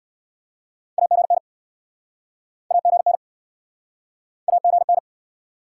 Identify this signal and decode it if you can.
Morse code, heard as a keyed tone, about 66 wpm, 700 Hz. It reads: AGN AGN AGN